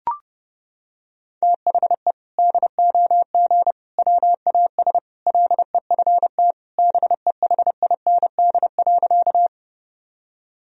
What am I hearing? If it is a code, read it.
Morse code, heard as a keyed tone, about 30 wpm, 700 Hz. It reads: THE DOG WAS LEFT BEHIND.
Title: Sentences from Top 1000 Words - Review Part 2 - 30wpm